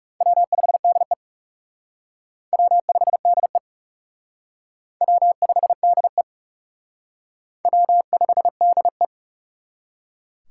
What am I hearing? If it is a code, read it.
Morse code, heard as a keyed tone, about 39 wpm, 700 Hz. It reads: W5DE W5DE W5DE W5DE